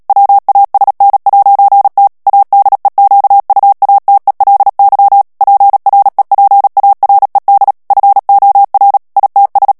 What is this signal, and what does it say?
Morse code, heard as a keyed tone, about 37 wpm, 800 Hz. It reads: WASN'T ADEQUATELY PREPARED FOR ITS
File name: morse mp3 / ANGLAIS texte clair 2 vitesse 35 mots minute